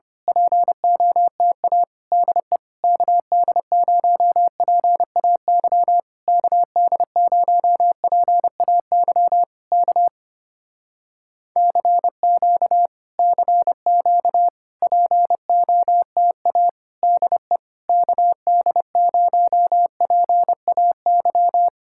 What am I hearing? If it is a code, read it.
Morse code, heard as a keyed tone, about 25 wpm, 700 Hz. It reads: POTA DE KD0PAY KD0PAY K CQ CQ POTA DE KD0PAY
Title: Speed-Racing Common QSO Phrases Encore - 25wpm